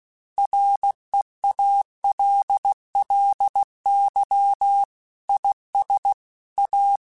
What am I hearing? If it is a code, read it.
Morse code, heard as a keyed tone, about 16 wpm, 800 Hz. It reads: REALLY IS A
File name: Technical Subjects / Technical Subjects-mp3s / Gates Intro @15